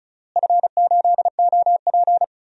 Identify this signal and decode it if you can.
Morse code, heard as a keyed tone, about 36 wpm, 700 Hz. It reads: F8OP
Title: DXCC-100 Call Signs - 4 Characters - Repeated 3x - 35wpm